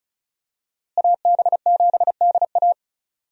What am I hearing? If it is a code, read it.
Morse code, heard as a keyed tone, about 34 wpm, 700 Hz. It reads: AB7DA
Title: USA Call Signs - 5 Characters - Repeated 3x - 35wpm